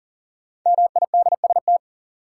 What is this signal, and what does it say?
Morse code, heard as a keyed tone, about 40 wpm, 700 Hz. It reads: MIDST
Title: Sets of 4 Words - New Word Drill - 40wpm